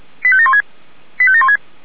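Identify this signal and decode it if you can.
Five-tone selective call (ZVEI), heard as quick runs of five tones, tones of about 70 ms. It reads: 85716 85716
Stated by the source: ZVEI five-tone selective call received on an SDR.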